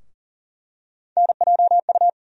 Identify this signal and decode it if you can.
Morse code, heard as a keyed tone, about 40 wpm, 700 Hz. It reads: NJU
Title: USA Call Sign Postfixes - 3 Letters 40wpm